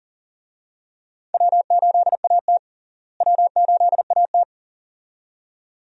Morse code, heard as a keyed tone, about 40 wpm, 700 Hz. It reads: W8AT W8AT
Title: USA Call Signs - 4 Characters - Repeated 2x - 40wpm